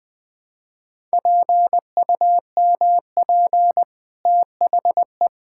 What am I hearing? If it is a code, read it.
Morse code, heard as a keyed tone, about 21 wpm, 700 Hz. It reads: PUMP THE
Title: Easy Sentences with Hints - 20wpm